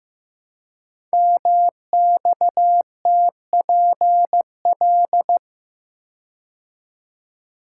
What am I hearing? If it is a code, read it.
Morse code, heard as a keyed tone, about 15 wpm, 700 Hz. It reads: MXTPL
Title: Code Groups - 5 Letters - 15wpm